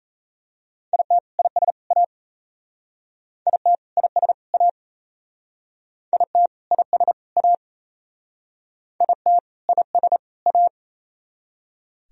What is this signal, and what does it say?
Morse code, heard as a keyed tone, about 37 wpm, 700 Hz. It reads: IT IS A IT IS A IT IS A IT IS A